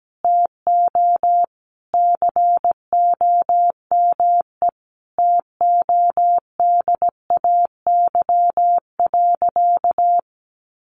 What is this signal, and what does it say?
Morse code, heard as a keyed tone, about 17 wpm, 700 Hz. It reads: TO COME TODAY.
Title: Additional Easy Phrases 17wpm